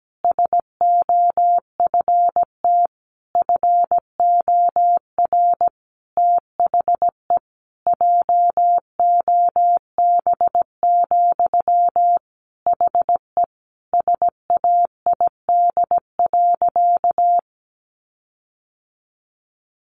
Morse code, heard as a keyed tone, about 17 wpm, 700 Hz. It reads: SOFT FOR THE JOB, HE SAID.
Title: Sentences from Top 1500 Words 17wpm